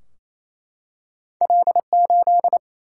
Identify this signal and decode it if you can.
Morse code, heard as a keyed tone, about 28 wpm, 700 Hz. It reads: L8
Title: Single-Letter Single-Number 28wpm